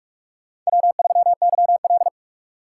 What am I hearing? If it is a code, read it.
Morse code, heard as a keyed tone, about 41 wpm, 700 Hz. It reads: W3YL